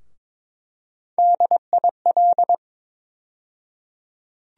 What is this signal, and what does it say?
Morse code, heard as a keyed tone, about 22 wpm, 700 Hz. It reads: DIL
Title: USA Call Sign Postfixes - 3 Letters 22wpm